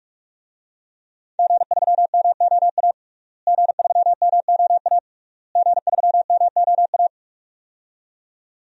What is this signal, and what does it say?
Morse code, heard as a keyed tone, about 46 wpm, 700 Hz. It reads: G3MOA G3MOA G3MOA
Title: DXCC-100 Call Signs - 5 Characters - Repeated 3x - 45wpm